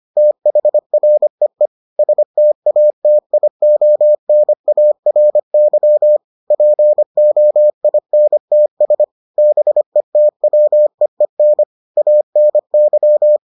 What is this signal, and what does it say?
Morse code, heard as a keyed tone, about 25 wpm, 600 Hz. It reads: THREE STATIONARY POINTS BETWEEN ANY